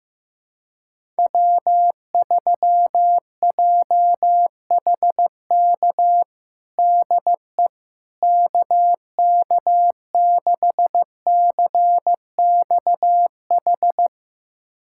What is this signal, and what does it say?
Morse code, heard as a keyed tone, about 15 wpm, 700 Hz. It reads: W3JHK DE KK6CXH
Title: USA Callsign DE Callsign 15wpm